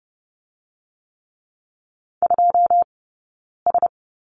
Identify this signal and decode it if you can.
Morse code, heard as a keyed tone, about 30 wpm, 700 Hz. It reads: 2 S
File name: Lesson 16d - Letter c - Two Characters 30wpm